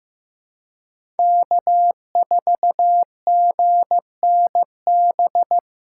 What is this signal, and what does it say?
Morse code, heard as a keyed tone, about 15 wpm, 700 Hz. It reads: K4GNB
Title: USA Callsign DE Callsign - 5 Characters - 15wpm